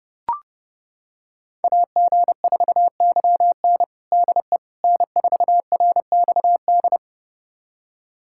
Morse code, heard as a keyed tone, about 30 wpm, 700 Hz. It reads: AG4YN DE N4RXD
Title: USA Callsign DE Callsign - 5 Characters - 30wpm